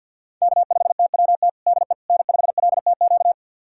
Morse code, heard as a keyed tone, about 51 wpm, 700 Hz. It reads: K5TWT DE N5LTQ
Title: USA Callsign DE Callsign - 5 Characters - 50wpm